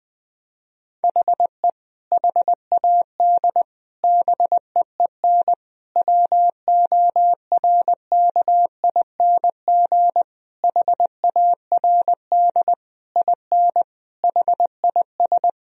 Morse code, heard as a keyed tone, about 20 wpm, 700 Hz. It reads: HE HAD BEEN WORKING HARD IN HIS
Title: Sentences from Top 1000 Words - Review Part 4 - 20wpm